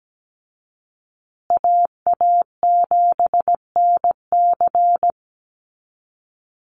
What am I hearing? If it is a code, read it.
Morse code, heard as a keyed tone, about 17 wpm, 700 Hz. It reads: AA7NC
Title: USA Call Signs Repeated 3x 17wpm